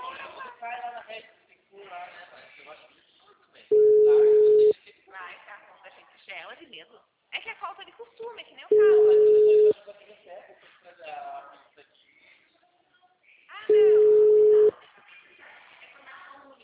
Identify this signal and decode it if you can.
Ringback tone (Europe, 425 Hz), with speech.